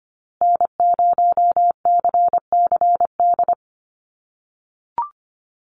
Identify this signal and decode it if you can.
Morse code, heard as a keyed tone, about 25 wpm, 700 Hz. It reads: N0CCD
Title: USA Call Signs Repeated 3x 25wpm